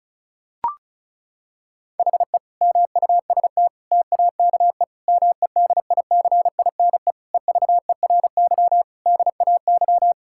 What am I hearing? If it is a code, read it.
Morse code, heard as a keyed tone, about 34 wpm, 700 Hz. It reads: HE MUST TAKE MEDICINE EVERY DAY
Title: Sentences from Top 1000 Words - Review Part 6 - 35wpm